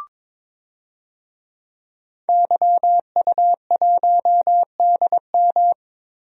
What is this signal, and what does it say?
Morse code, heard as a keyed tone, about 22 wpm, 700 Hz. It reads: YU1DM
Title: DXCC-100 Call Signs - 5 Characters - Repeated 3x - 22wpm